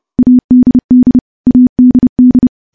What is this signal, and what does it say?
Morse code, heard as a keyed tone, about 30 wpm, 250 Hz. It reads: ADD ADD
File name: morse / morse-training-30WPM